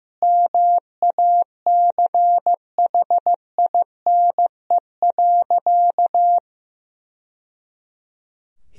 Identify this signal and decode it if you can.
Morse code, heard as a keyed tone, about 15 wpm, 700 Hz. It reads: MACHINE.
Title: Sentences from Top 1000 Words - Review Part 2 - 15wpm